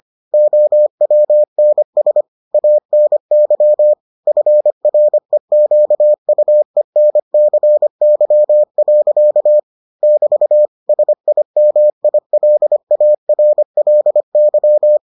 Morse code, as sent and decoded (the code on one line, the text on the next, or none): --- .-- -. ... / .- -. -.-- / ..-. .-. --.- ..- . -. -.-. -.-- .-.-.- / -...- / ... .. -- .. .-.. .- .-. .-.. -.--
OWNS ANY FREQUENCY. <BT> SIMILARLY